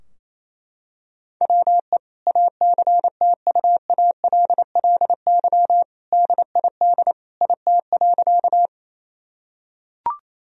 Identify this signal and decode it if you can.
Morse code, heard as a keyed tone, about 28 wpm, 700 Hz. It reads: WE ACTUALLY DID IT.
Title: Sentences from Top 1000 Words - Review Part 4 - 28wpm